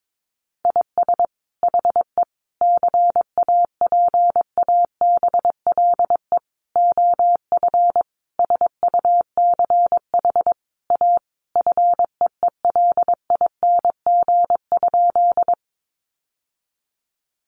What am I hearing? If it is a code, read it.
Morse code, heard as a keyed tone, about 22 wpm, 700 Hz. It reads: IS HE CAPABLE OF SUCH A FEELING?
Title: Sentences from Top 1200 Words 22wpm